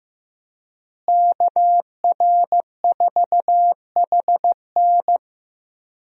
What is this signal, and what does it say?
Morse code, heard as a keyed tone, about 15 wpm, 700 Hz. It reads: KR4HN
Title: USA Call Signs - 5 Characters - Repeated 3x - 15wpm